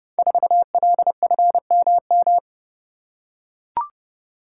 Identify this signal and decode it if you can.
Morse code, heard as a keyed tone, about 30 wpm, 700 Hz. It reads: 4LFMM
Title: Code Groups - 5 Letter:Numbers - 30wpm